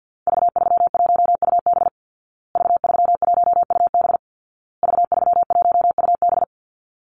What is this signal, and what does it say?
Morse code, heard as a keyed tone, about 50 wpm, 700 Hz. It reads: V31UB V31UB V31UB
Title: Call Signs Repeated 2x - 50wpm